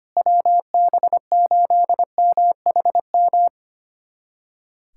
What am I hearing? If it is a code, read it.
Morse code, heard as a keyed tone, about 25 wpm, 700 Hz. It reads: WB8MHM
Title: USA Callsign DE Callsign - 6 Characters - 25wpm